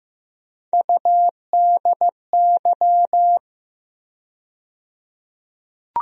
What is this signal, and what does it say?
Morse code, heard as a keyed tone, about 15 wpm, 700 Hz. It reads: UDY
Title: USA Call Sign Postfixes - 3 Letters 15wpm